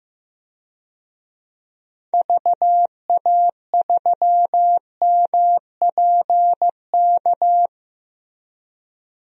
Morse code, heard as a keyed tone, about 16 wpm, 700 Hz. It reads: VA3MPK
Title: DXCC-100 Call Signs - 6 Characters - Repeated 2x - 15wpm